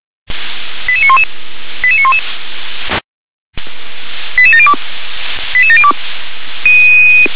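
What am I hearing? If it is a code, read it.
Five-tone selective call (ZVEI), heard as quick runs of five tones, tones of about 80 ms. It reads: 88911 88911 88702 88702
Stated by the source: ZVEI five-tone selective call received on SDR.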